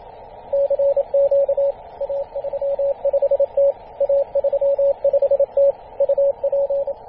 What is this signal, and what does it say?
Morse code, heard as a keyed tone, about 27 wpm, 560 Hz, with radio noise. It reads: CQ A35T A35T UP
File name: posts / 2016-feb-27-0448z-a35t-24-898mhz-cw